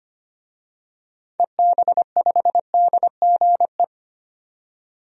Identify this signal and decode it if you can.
Morse code, heard as a keyed tone, about 25 wpm, 700 Hz. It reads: EB5DGE
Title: DXCC-100 Call Signs - 6 Characters - 25wpm